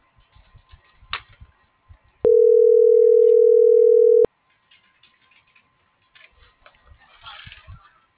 Ringback tone (North America), 440 + 480 Hz.